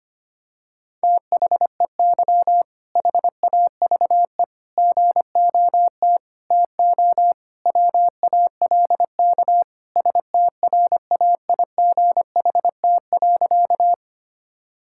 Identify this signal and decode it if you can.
Morse code, heard as a keyed tone, about 25 wpm, 700 Hz. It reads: THEY HAVE GOT TO WALK STRAIGHT.
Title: Sentences from Top 1000 Words - Review Part 6 - 25wpm